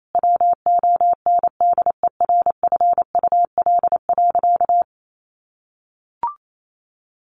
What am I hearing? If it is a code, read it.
Morse code, heard as a keyed tone, about 28 wpm, 700 Hz. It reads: WONDERFUL.
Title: Sentences from Top 1100 Words 28wpm